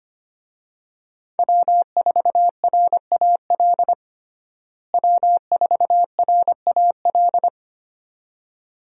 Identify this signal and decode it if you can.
Morse code, heard as a keyed tone, about 25 wpm, 700 Hz. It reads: W4RAL W4RAL
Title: USA Call Signs - 5 Characters - Repeated 2x - 25wpm